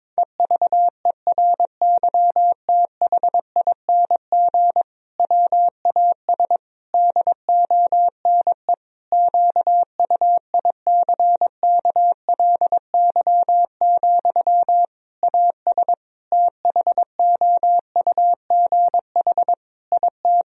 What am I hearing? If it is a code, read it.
Morse code, heard as a keyed tone, about 22 wpm, 700 Hz. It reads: EVERYTHING WAS DONE QUICKLY, AS THOUGH IT
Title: Sentences from Top 1000 Words - Review Part 2 - 22wpm